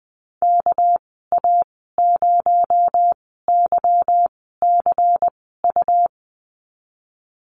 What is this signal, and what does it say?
Morse code, heard as a keyed tone, about 20 wpm, spaced as for 15 wpm, 700 Hz. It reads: KA0YCU
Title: 219 Lesson 35g - Number 0 - Call Signs 20wpm